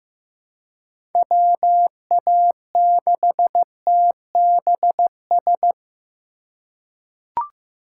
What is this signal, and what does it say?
Morse code, heard as a keyed tone, about 15 wpm, 700 Hz. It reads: WA6TBS
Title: USA Call Signs - 2x Character Spacing - 15wpm